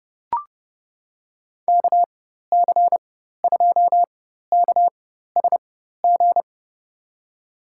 Morse code, heard as a keyed tone, about 30 wpm, spaced as for 14 wpm, 700 Hz. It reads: KC2KSG
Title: USA Call Signs - 6 Characters - 4x Character Spacing - 30wpm